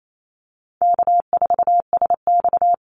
Morse code, heard as a keyed tone, about 28 wpm, 700 Hz. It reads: K4SX
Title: Easy Call Signs Repeated 2x - 28wpm